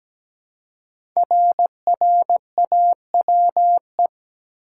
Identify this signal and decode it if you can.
Morse code, heard as a keyed tone, about 17 wpm, 700 Hz. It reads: RRAWE
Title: Code Groups - 5 Letters - 17wpm